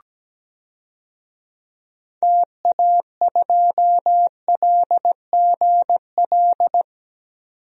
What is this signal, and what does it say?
Morse code, heard as a keyed tone, about 17 wpm, 700 Hz. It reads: TA2LGL
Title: DXCC-100 Call Signs - 6 Characters - 17wpm